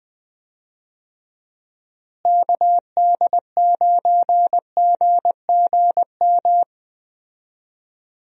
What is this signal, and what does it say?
Morse code, heard as a keyed tone, about 21 wpm, 700 Hz. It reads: KD9GGM
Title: USA Call Signs - 6 Characters - Repeated 3x - 20wpm